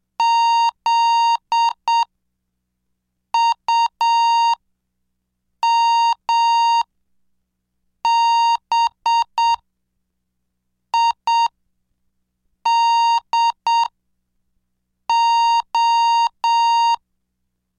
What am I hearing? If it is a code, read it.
Morse code, heard as a keyed tone, about 7 wpm, spaced as for 5 wpm, 930 Hz. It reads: ZUMBIDO
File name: cursos / CDestudo / 17 Track 17